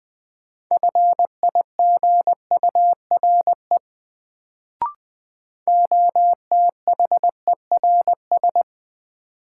Morse code, heard as a keyed tone, about 21 wpm, 700 Hz. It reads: FIGURE OTHERS
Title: Sentences from Top 1000 Words - Review Part 2 - New Word Drill - 20wpm